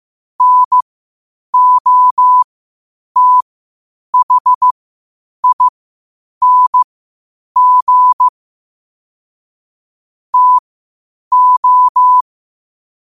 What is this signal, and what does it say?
Morse code, heard as a keyed tone, about 15 wpm, spaced as for 8 wpm, 1000 Hz. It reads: NOTHING TO
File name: qotdfiles / 8wpm